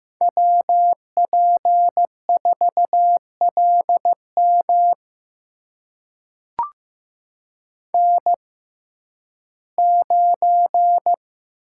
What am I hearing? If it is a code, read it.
Morse code, heard as a keyed tone, about 15 wpm, spaced as for 11 wpm, 700 Hz. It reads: WP4LM N9
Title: USA Call Signs - 5 Characters - 6x Character Spacing - 15wpm